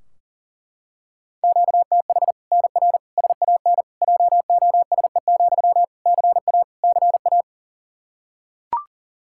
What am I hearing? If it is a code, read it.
Morse code, heard as a keyed tone, about 40 wpm, 700 Hz. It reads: QTH NR SAN JOSE, CA CA
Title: Practice QSO - Exchange 1 - Beginning - 40wpm